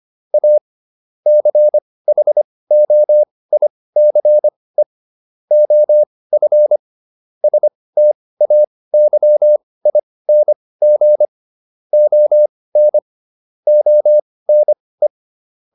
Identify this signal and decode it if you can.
Morse code, heard as a keyed tone, about 25 wpm, spaced as for 18 wpm, 600 Hz. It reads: A CHOICE OF STAYING ON ONE